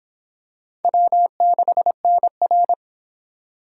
Morse code, heard as a keyed tone, about 25 wpm, 700 Hz. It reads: W6NR